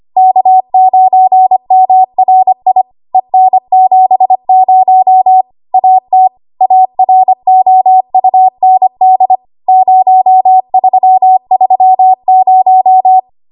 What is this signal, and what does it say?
Morse code, heard as a keyed tone, about 25 wpm, 750 Hz. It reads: K9MRI EN70 AT AROUND 0330
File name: Morse / 25WPM